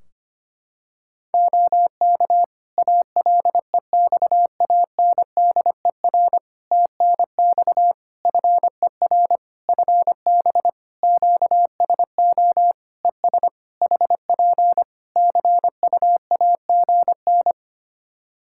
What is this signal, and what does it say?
Morse code, heard as a keyed tone, about 25 wpm, 700 Hz. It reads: OK ALEXANDER TNX FER FB QSO ES HP CUAGN
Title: K1USN SST QSO Practice 25wpm